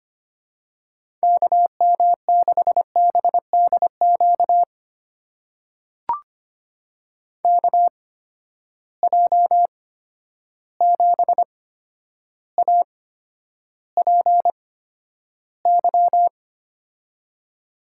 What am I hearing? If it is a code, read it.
Morse code, heard as a keyed tone, about 25 wpm, spaced as for 11 wpm, 700 Hz. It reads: KM6BDQ KJ7APY
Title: USA Call Signs - 6 Characters - 8x Character Spacing - 25wpm